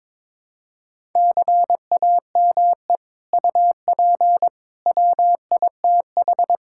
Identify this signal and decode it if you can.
Morse code, heard as a keyed tone, about 22 wpm, 700 Hz. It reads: CAME UP WITH
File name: practice / 22wpm / Sets of 6 Words - 22wpm